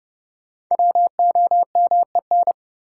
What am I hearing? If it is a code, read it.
Morse code, heard as a keyed tone, about 30 wpm, 700 Hz. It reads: WOMEN
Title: Sets of 3 Words - New Word Drill - 30wpm